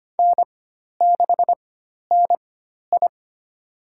Morse code, heard as a keyed tone, about 25 wpm, spaced as for 12 wpm, 700 Hz. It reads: N6NI
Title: USA Call Signs - 4x Character Spacing - 25wpm